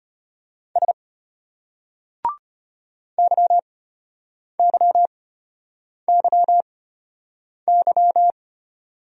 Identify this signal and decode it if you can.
Morse code, heard as a keyed tone, about 32 wpm, spaced as for 8 wpm, 700 Hz. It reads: S YYYY